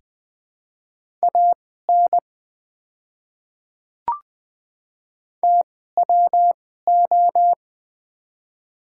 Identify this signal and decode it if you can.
Morse code, heard as a keyed tone, about 21 wpm, spaced as for 15 wpm, 700 Hz. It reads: AN TWO
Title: 2x Character Spacing - Top 100 Words - 20wpm